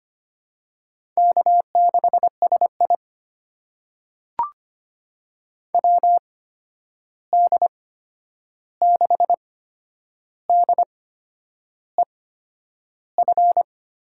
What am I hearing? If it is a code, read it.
Morse code, heard as a keyed tone, about 25 wpm, spaced as for 9 wpm, 700 Hz. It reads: K6SI WD6DEF
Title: USA Call Signs - 8x Character Spacing - 25wpm